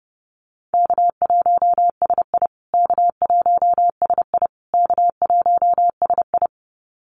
Morse code, heard as a keyed tone, about 30 wpm, 700 Hz. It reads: K1SI K1SI K1SI
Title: USA Call Signs Repeated 3x 30wpm